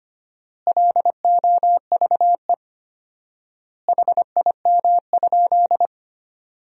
Morse code, heard as a keyed tone, about 25 wpm, 700 Hz. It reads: LOVE HIM?
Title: 4x Word-Spacing - Sentences from Top 400 Words - 25wpm